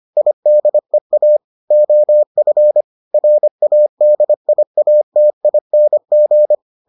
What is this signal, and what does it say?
Morse code, heard as a keyed tone, about 25 wpm, 600 Hz. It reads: IDEA OF RADIATING